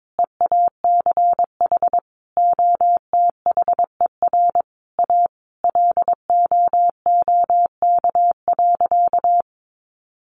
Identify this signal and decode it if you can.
Morse code, heard as a keyed tone, about 22 wpm, 700 Hz. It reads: EACH OTHER A LOOK.
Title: A Sound Of Thunder 22wpm